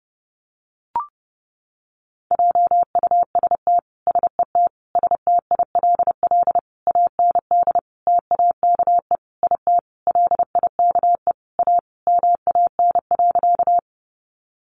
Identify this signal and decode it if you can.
Morse code, heard as a keyed tone, about 30 wpm, 700 Hz. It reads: JUST SET STILL AND TAKE IT LIKE A MAN.
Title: Sentences from Top 300 Words 30wpm